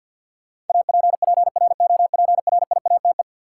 Morse code, heard as a keyed tone, about 52 wpm, 700 Hz. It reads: APPROPRIATE
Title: Sentences from Top 900 Words - New Word Drill - 50wpm